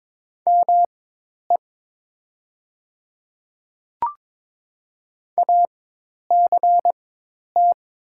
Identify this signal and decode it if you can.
Morse code, heard as a keyed tone, about 22 wpm, spaced as for 10 wpm, 700 Hz. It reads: ME ACT
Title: 4x Character Spacing - Top 200 Words - 22wpm